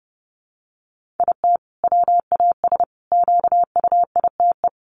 Morse code, heard as a keyed tone, about 30 wpm, 700 Hz. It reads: IT WAS QUITE
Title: Sentences from Top 1500 Words 30wpm